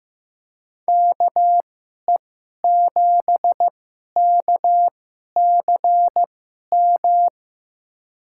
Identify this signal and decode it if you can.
Morse code, heard as a keyed tone, about 15 wpm, spaced as for 11 wpm, 700 Hz. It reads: KE7KCM
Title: USA Call Signs - 6 Characters - 2x Character Spacing - 15wpm